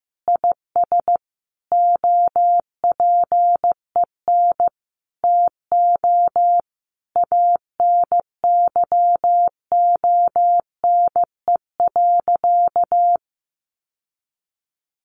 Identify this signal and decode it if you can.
Morse code, heard as a keyed tone, about 15 wpm, 700 Hz. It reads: IS OPEN TO ANYONE.
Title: Sentences from Top 1200 Words 15wpm